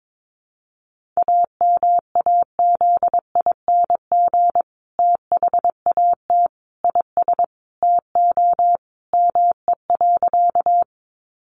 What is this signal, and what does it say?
Morse code, heard as a keyed tone, about 22 wpm, 700 Hz. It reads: AMAZING THAT IS TO ME.
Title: Sentences from Top 1500 Words 22wpm